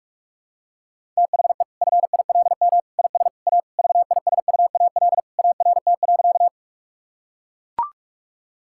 Morse code, heard as a keyed tone, about 41 wpm, 700 Hz. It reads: THE FILM IS A VISUAL ART.
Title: Sentences from Top 1000 Words - Review Part 6 - 45wpm